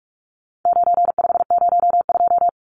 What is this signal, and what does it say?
Morse code, heard as a keyed tone, about 44 wpm, 700 Hz. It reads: 9502
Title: Numbers - 4-Digit 45wpm